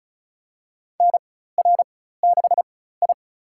Morse code, heard as a keyed tone, about 36 wpm, spaced as for 16 wpm, 700 Hz. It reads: NR6I
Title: USA Call Signs - 4 Characters - 4x Character Spacing - 35wpm